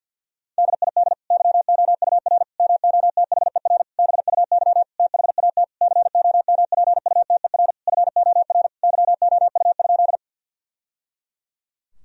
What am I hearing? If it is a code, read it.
Morse code, heard as a keyed tone, about 47 wpm, 700 Hz. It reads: DID YOUR MOTHER BUY THAT COMPUTER FOR YOU?
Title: Sentences from Top 1000 Words - Review Part 3 - 50wpm